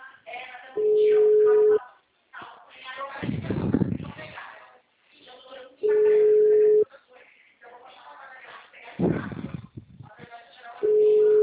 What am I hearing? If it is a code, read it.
Ringback tone (Europe, 425 Hz), with speech.